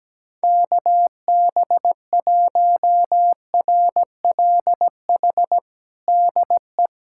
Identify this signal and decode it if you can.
Morse code, heard as a keyed tone, about 17 wpm, 700 Hz. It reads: KB1RLH DE
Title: USA Callsign DE Callsign 17wpm